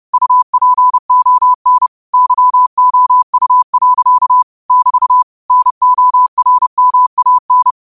Morse code, heard as a keyed tone, about 30 wpm, 1000 Hz. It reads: UPON YOU. <BT> NORMAN
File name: qotdfiles / 30wpm